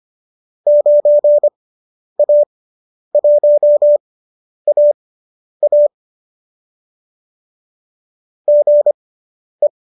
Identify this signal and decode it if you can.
Morse code, heard as a keyed tone, about 25 wpm, spaced as for 10 wpm, 590 Hz. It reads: 9A1AA GE
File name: qso219 10